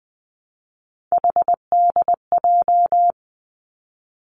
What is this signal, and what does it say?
Morse code, heard as a keyed tone, about 20 wpm, 700 Hz. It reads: HDJ
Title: Three Challenging Letter-Number Combinations 20wpm